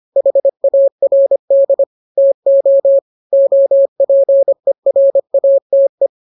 Morse code, heard as a keyed tone, about 25 wpm, 550 Hz. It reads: HARD TO OPERATE
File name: jokes / joke 9 25 0000